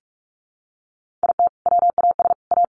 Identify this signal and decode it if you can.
Morse code, heard as a keyed tone, about 44 wpm, 700 Hz. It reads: IT WAS A
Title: Sentences from Top 1100 Words 45wpm